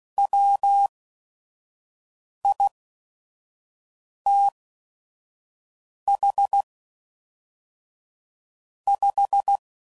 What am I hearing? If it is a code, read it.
Morse code, heard as a keyed tone, about 16 wpm, spaced as for 5 wpm, 800 Hz. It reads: WITH 5
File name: CW Code Groups @5WPM / Technical Subjects / AND Gate